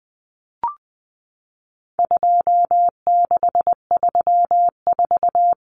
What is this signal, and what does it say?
Morse code, heard as a keyed tone, about 20 wpm, 700 Hz. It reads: 2634
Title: Numbers - 4-Digit 20wpm